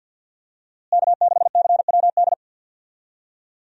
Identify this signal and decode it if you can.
Morse code, heard as a keyed tone, about 48 wpm, 700 Hz. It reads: K6CWD
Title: USA Call Signs - 5 Characters - Repeated 2x - 50wpm